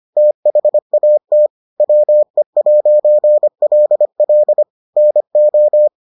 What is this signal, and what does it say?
Morse code, heard as a keyed tone, about 25 wpm, 600 Hz. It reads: THAT WE'LL NO